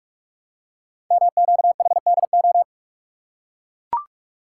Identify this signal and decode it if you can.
Morse code, heard as a keyed tone, about 44 wpm, 690 Hz. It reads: MQHDO
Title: Code Groups - 5 Letters - 45wpm